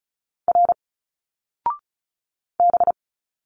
Morse code, heard as a keyed tone, about 36 wpm, 700 Hz. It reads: R B
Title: Single Letters 35wpm